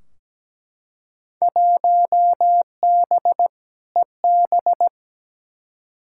Morse code, heard as a keyed tone, about 17 wpm, 700 Hz. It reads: 1B EB
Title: Field Day Exchanges - 17wpm